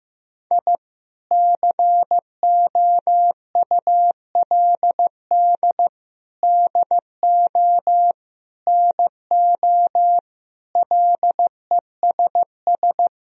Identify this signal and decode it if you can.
Morse code, heard as a keyed tone, about 15 wpm, 700 Hz. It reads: I COULD DO NO LESS
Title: Sentences from Top 1000 Words - Review Part 3 - 15wpm